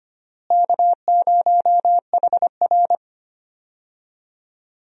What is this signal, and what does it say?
Morse code, heard as a keyed tone, about 25 wpm, 700 Hz. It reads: K0HR
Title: USA Call Signs - 4 Characters - Repeated 3x - 25wpm